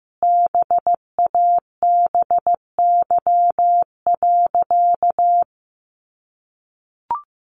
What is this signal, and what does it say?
Morse code, heard as a keyed tone, about 14 wpm, 700 Hz. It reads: BABY.
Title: Sentences from Top 1300 Words 15wpm